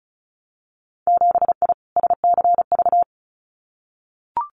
Morse code, heard as a keyed tone, about 36 wpm, 700 Hz. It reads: 7I SCV
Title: Winter Field Day Exchanges Encore 35wpm